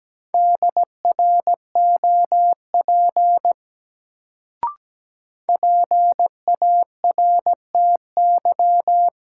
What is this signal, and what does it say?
Morse code, heard as a keyed tone, about 17 wpm, 700 Hz. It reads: DROP PARTY
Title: Sentences from Top 1400 Words - New Word Drill Alt - 17wpm